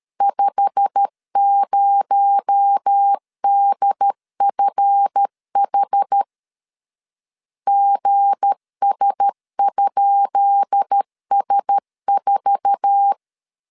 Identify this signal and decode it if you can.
Morse code, heard as a keyed tone, about 13 wpm, 800 Hz. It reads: 50DFH GS?S4
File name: LEZIONI / 10 SIGNS / 10D COMA 12wpm 2544